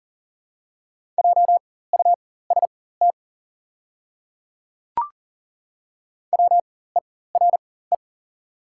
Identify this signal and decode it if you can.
Morse code, heard as a keyed tone, about 40 wpm, spaced as for 19 wpm, 690 Hz. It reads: JUST WERE
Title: 4x Character Spacing - Top 200 Words - 40wpm